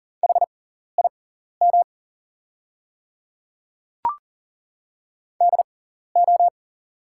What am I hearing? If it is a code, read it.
Morse code, heard as a keyed tone, about 40 wpm, spaced as for 14 wpm, 700 Hz. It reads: HIM DO
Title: 6x Character Spacing - Top 200 Words - 40wpm